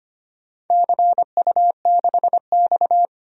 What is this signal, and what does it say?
Morse code, heard as a keyed tone, about 25 wpm, 700 Hz. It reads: CU6X
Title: DXCC-100 Call Signs - 4 Characters - 25wpm